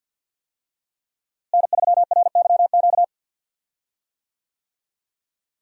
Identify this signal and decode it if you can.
Morse code, heard as a keyed tone, about 51 wpm, 700 Hz. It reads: N3RYQ
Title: USA Call Signs - 5 Characters - Repeated 2x - 50wpm